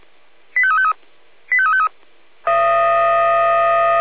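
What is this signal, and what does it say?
Five-tone selective call (ZVEI), heard as quick runs of five tones, tones of about 70 ms. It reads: 85353 85353
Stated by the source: Five-tone ZVEI selective call 85320, received by SDR.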